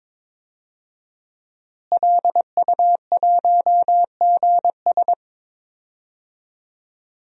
Morse code, heard as a keyed tone, about 22 wpm, 700 Hz. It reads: LU1GS
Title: DXCC-100 Call Signs - 5 Characters - Repeated 3x - 22wpm